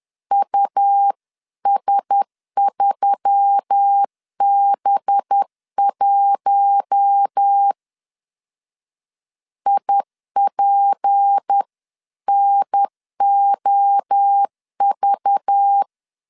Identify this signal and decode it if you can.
Morse code, heard as a keyed tone, about 10 wpm, 800 Hz. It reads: US3B1 IPNOV
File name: CW / DET / 1 6 1500C S B 10wpm 4137